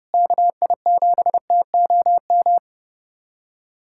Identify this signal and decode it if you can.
Morse code, heard as a keyed tone, about 32 wpm, 700 Hz. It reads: KI7TOM
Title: USA Callsign DE Callsign - 6 Characters - 30wpm